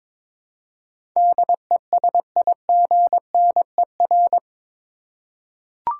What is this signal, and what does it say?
Morse code, heard as a keyed tone, about 22 wpm, 700 Hz. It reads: DESIGNER
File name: practice / 22wpm / Eight Letter Words - 22wpm